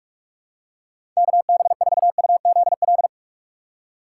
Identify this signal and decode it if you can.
Morse code, heard as a keyed tone, about 43 wpm, 700 Hz. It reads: KB4UZL
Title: USA Call Signs - 6 Characters - 2x Character Spacing - 45wpm